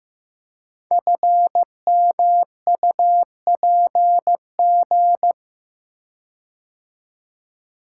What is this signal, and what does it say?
Morse code, heard as a keyed tone, about 15 wpm, 700 Hz. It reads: FMUPG
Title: Code Groups - 5 Letters - 15wpm